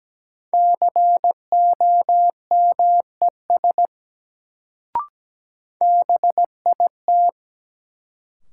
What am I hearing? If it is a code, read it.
Morse code, heard as a keyed tone, about 17 wpm, 700 Hz. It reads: COMES BIT
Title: Sets of 3 Words - New Word Drill - 17wpm